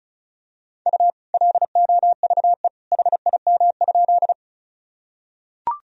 Morse code, heard as a keyed tone, about 34 wpm, 700 Hz. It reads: U LOVE HIM?
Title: Sentences from Top 400 Words and Abbreviations 35wpm